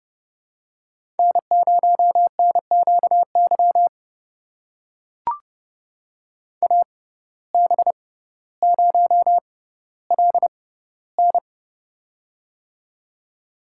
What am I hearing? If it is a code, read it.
Morse code, heard as a keyed tone, about 30 wpm, spaced as for 15 wpm, 700 Hz. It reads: N0NQY AB0LN
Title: USA Call Signs - 5 Characters - 6x Character Spacing - 30wpm